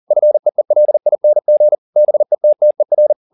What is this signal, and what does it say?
Morse code, heard as a keyed tone, about 40 wpm, 600 Hz. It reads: FEELING BETTER
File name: the old man and the sea 2 40wpm